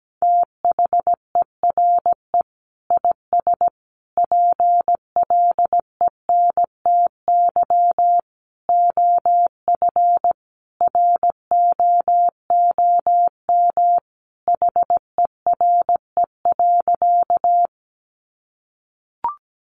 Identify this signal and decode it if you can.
Morse code, heard as a keyed tone, about 17 wpm, 700 Hz. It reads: THERE IS PLENTY OF ROOM HERE.
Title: Sentences from Top 1300 Words 17wpm